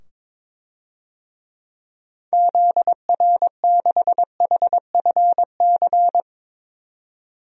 Morse code, heard as a keyed tone, about 22 wpm, 700 Hz. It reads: ZR6HFC
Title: DXCC-100 Call Signs - 6 Characters - 22wpm